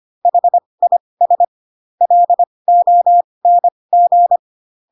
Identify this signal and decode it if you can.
Morse code, heard as a keyed tone, about 25 wpm, spaced as for 20 wpm, 700 Hz. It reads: HIS LONG